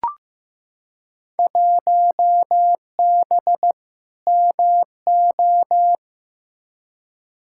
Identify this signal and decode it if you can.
Morse code, heard as a keyed tone, about 15 wpm, 700 Hz. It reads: 1B MO
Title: Field Day Exchanges - 15wpm